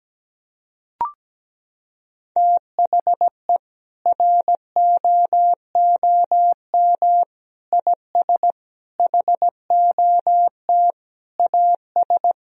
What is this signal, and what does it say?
Morse code, heard as a keyed tone, about 17 wpm, 700 Hz. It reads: THE ROOM IS HOT AS
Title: Sentences from Top 1000 Words - Review Part 4 - 17wpm